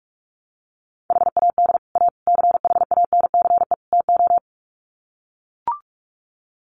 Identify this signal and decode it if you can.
Morse code, heard as a keyed tone, about 44 wpm, 700 Hz. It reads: HAD A CHANCE TO
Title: Sets of 4 Words 45wpm